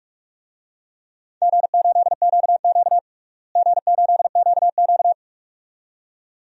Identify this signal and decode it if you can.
Morse code, heard as a keyed tone, about 41 wpm, 700 Hz. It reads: G8QQ G8QQ
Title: DXCC-100 Call Signs - 4 Characters - Repeated 2x - 45wpm